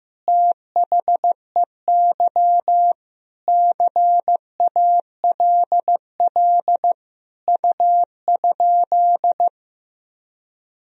Morse code, heard as a keyed tone, about 15 wpm, 700 Hz. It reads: THEY CALL U?
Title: Sentences from Top 100 Words and Abbreviations 15wpm